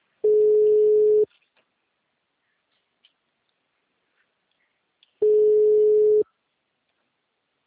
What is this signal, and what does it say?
Ringback tone (Europe, 425 Hz).
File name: out- -2004- - - .1019